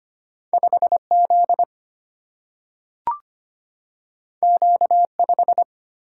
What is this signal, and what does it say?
Morse code, heard as a keyed tone, about 25 wpm, 700 Hz. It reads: 5Z Q5
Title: Single-Letter Single-Number 25wpm